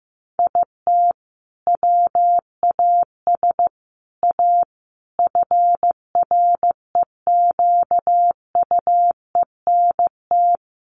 Morse code, heard as a keyed tone, about 14 wpm, 700 Hz. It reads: IT WAS A FREQUENT
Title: Sentences from Top 1400 Words 15wpm